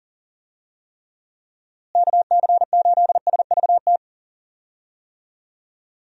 Morse code, heard as a keyed tone, about 40 wpm, 700 Hz. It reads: KC8SVT
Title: USA Call Signs - 6 Characters - Repeated 3x - 40wpm